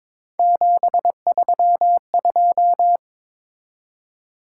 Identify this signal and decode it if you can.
Morse code, heard as a keyed tone, about 21 wpm, 700 Hz. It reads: 732
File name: practice / 22wpm / Three Letter-Number Combinations - 22wpm